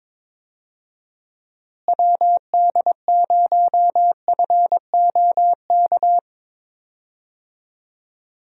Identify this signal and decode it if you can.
Morse code, heard as a keyed tone, about 22 wpm, 700 Hz. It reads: WD0FOK
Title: USA Call Signs - 6 Characters - Repeated 3x - 22wpm